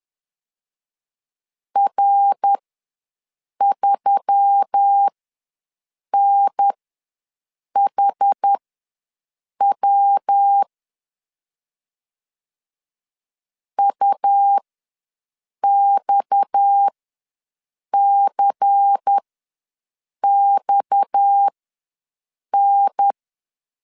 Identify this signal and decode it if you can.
Morse code, heard as a keyed tone, about 10 wpm, spaced as for 6 wpm, 800 Hz. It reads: R3NHW UXCXN
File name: LEZIONI / 6 PKX0 / 6 PKX ws 10wpm 3706